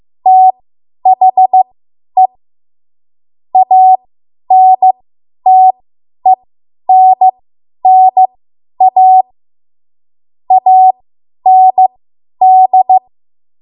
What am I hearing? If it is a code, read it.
Morse code, heard as a keyed tone, about 15 wpm, spaced as for 10 wpm, 750 Hz. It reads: THE ANTENNA AND
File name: Archive / 10 WPM / 10WPM